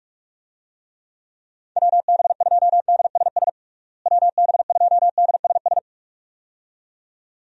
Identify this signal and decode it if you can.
Morse code, heard as a keyed tone, about 42 wpm, 700 Hz. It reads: WB2DSS WB2DSS
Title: USA Call Signs - 6 Characters - Repeated 2x - 45wpm